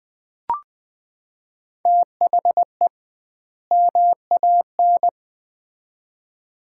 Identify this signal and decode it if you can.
Morse code, heard as a keyed tone, about 20 wpm, 700 Hz. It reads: THE MAN
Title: 2x Word-Spacing - Sets of 2 Words - 20wpm